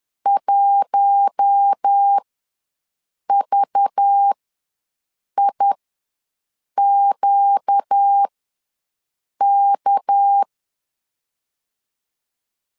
Morse code, heard as a keyed tone, about 10 wpm, spaced as for 6 wpm, 800 Hz. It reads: 1VIQK
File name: LEZIONI / 6 PKX0 / 6 PK ws 10wpm 3745